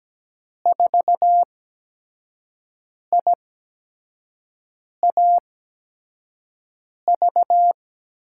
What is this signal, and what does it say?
Morse code, heard as a keyed tone, about 17 wpm, spaced as for 5 wpm, 700 Hz. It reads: 4IAV
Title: USA Call Signs - 8x Character Spacing - 17wpm